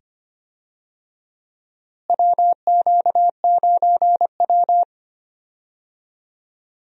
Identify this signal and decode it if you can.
Morse code, heard as a keyed tone, about 25 wpm, 700 Hz. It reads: WQ9W
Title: USA Call Signs - 4 Characters - Repeated 2x - 25wpm